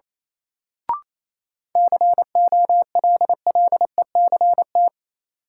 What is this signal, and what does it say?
Morse code, heard as a keyed tone, about 28 wpm, 700 Hz. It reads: COLLECT
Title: Sentences from Top 1600 Words - New Word Drill - 28wpm